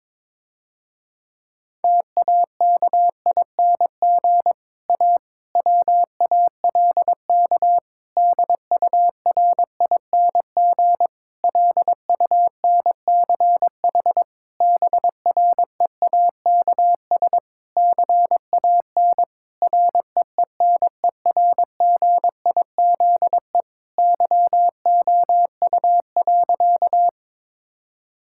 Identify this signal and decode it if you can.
Morse code, heard as a keyed tone, about 21 wpm, 700 Hz. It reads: TAKING A WALK DURING LUNCH BREAKS CAN REENERGIZE YOU.
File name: Difficult Sentences with Hints - 22wpm